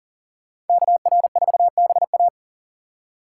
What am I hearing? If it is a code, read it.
Morse code, heard as a keyed tone, about 52 wpm, 700 Hz. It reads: KR4BA